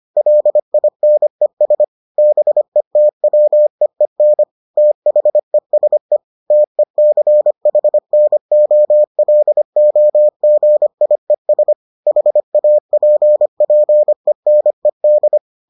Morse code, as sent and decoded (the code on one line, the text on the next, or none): .-.. .. -. ... / -... - .-- . -. / - .... ... / - . -.-. .... -. --- .-.. --- --. .. ... / .... .- .--. .--. -. -..
LINES BETWEEN THESE TECHNOLOGIES HAPPENED